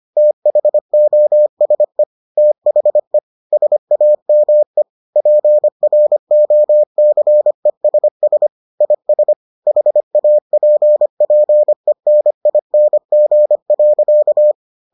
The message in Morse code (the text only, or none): THOSE THE SAME PROCESS IS HAPPENING.